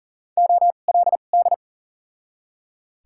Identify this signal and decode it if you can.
Morse code, heard as a keyed tone, about 40 wpm, spaced as for 29 wpm, 700 Hz. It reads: OLD
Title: 2x Character Spacing - Top 300 Words - 40wpm